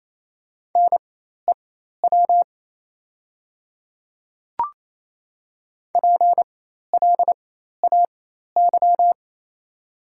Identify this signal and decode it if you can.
Morse code, heard as a keyed tone, about 28 wpm, spaced as for 13 wpm, 690 Hz. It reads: NEW PLAY
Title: 4x Character Spacing - Top 200 Words - 28wpm